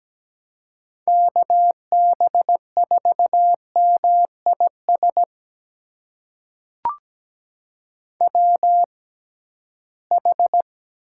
Morse code, heard as a keyed tone, about 17 wpm, spaced as for 13 wpm, 700 Hz. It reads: KB4MIS WH
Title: USA Call Signs - 6 Characters - 6x Character Spacing - 17wpm